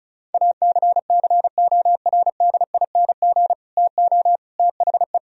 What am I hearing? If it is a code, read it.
Morse code, heard as a keyed tone, about 42 wpm, 700 Hz. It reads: ACCORDING TO THE
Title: Speed-Racing - Sets of 3 Words 35wpm